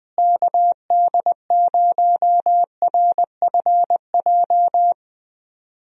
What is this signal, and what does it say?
Morse code, heard as a keyed tone, about 20 wpm, 700 Hz. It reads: KD0RFJ
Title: USA Call Signs - 6 Characters - Repeated 3x - 20wpm